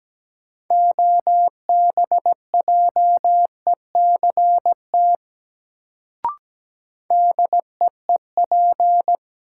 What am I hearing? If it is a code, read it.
Morse code, heard as a keyed tone, about 17 wpm, 700 Hz. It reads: OBJECT DEEP
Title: Sentences from Top 500 Words - New Word Drill - 17wpm